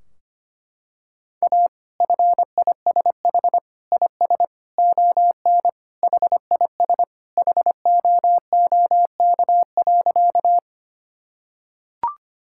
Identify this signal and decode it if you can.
Morse code, heard as a keyed tone, about 25 wpm, 700 Hz. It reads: A FISH IS ON HIS HOOK.
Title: Sentences from Top 1600 Words 25wpm